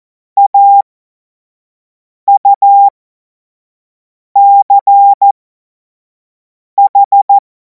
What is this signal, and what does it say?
Morse code, heard as a keyed tone, about 14 wpm, spaced as for 5 wpm, 800 Hz. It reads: AUCH
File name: Morsekurs / Witze / Witz 25. August 2026 25 F70